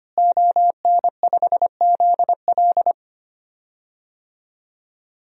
Morse code, heard as a keyed tone, about 25 wpm, 700 Hz. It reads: ON5ZL
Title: DXCC-100 Call Signs - 5 Characters - Repeated 2x - 25wpm